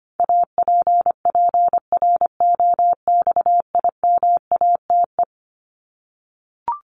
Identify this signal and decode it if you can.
Morse code, heard as a keyed tone, about 25 wpm, 700 Hz. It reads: APPROXIMATE
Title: LY Suffix - Top 100 Words with Base Words 25wpm